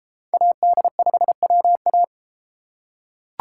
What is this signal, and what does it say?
Morse code, heard as a keyed tone, about 31 wpm, 700 Hz. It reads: AD5WA